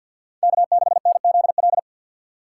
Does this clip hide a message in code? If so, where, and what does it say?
Yes: Morse code from 0.4 to 1.8 s, heard as a keyed tone, about 64 wpm, 700 Hz. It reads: K6NZL